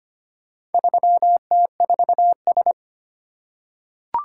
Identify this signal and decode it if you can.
Morse code, heard as a keyed tone, about 25 wpm, 700 Hz. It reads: 3T4S
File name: practice / 25wpm / Four Letter-Number Combinations - 25wpm